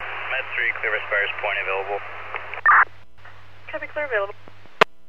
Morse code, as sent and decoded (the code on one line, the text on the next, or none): none